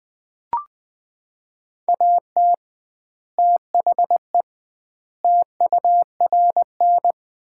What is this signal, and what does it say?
Morse code, heard as a keyed tone, about 20 wpm, 700 Hz. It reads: AT THE TURN
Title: 2x Word-Spacing - Sets of 5 Words - 20wpm